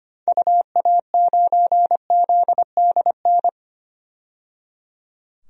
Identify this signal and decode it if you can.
Morse code, heard as a keyed tone, about 25 wpm, 700 Hz. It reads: UA9ZDN
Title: DXCC-100 Call Signs - 6 Characters - 25wpm